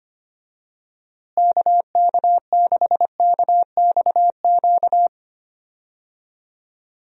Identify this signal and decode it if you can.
Morse code, heard as a keyed tone, about 25 wpm, 700 Hz. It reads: KK6KXQ
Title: USA Call Signs - 6 Characters - Repeated 3x - 25wpm